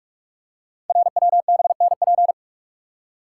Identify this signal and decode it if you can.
Morse code, heard as a keyed tone, about 44 wpm, 690 Hz. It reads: RWBNP
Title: Code Groups - 5 Letters - 45wpm